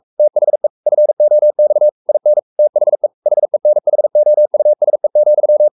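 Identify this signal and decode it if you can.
Morse code, heard as a keyed tone, about 44 wpm, 600 Hz. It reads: THE FOX IN THE HENHOUSE,
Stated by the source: Morse code at 43 wpm sending Ken Follett's novel Never in English, from CW med Gnister.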